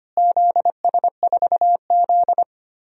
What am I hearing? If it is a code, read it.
Morse code, heard as a keyed tone, about 25 wpm, 700 Hz. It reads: ZS4Z
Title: DXCC-100 Call Signs - 4 Characters - Repeated 2x - 25wpm